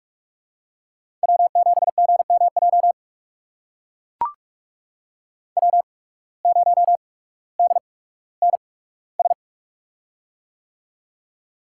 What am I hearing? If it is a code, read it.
Morse code, heard as a keyed tone, about 44 wpm, spaced as for 19 wpm, 700 Hz. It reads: W7GMJ W0DNS
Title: USA Call Signs - 8x Character Spacing - 45wpm